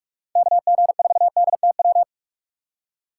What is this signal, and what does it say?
Morse code, heard as a keyed tone, about 46 wpm, 700 Hz. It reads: KG4DTW
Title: USA Call Signs - 6 Characters - 6x Character Spacing - 45wpm